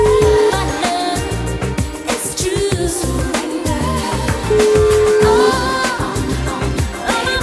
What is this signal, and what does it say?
Ringback tone (Europe, 425 Hz), with noise.